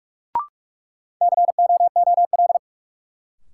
Morse code, heard as a keyed tone, about 44 wpm, 700 Hz. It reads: COOL
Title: Sentences from Top 1200 Words - New Word Drill - 45wpm